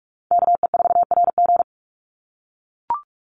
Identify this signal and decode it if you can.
Morse code, heard as a keyed tone, about 44 wpm, 700 Hz. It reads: KE4RG
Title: Call Signs - 45wpm